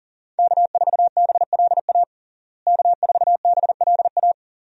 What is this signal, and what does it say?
Morse code, heard as a keyed tone, about 40 wpm, 700 Hz. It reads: K4BLA K4BLA
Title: USA Call Signs - 5 Characters - Repeated 3x - 40wpm